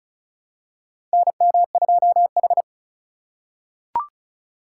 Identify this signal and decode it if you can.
Morse code, heard as a keyed tone, about 34 wpm, 700 Hz. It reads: NM2H
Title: USA Call Signs - 4 Characters - 2x Character Spacing - 35wpm